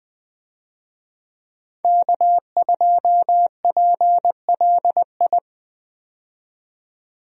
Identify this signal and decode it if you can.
Morse code, heard as a keyed tone, about 20 wpm, 700 Hz. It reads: K2PLI
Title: USA Call Signs - 5 Characters - 20wpm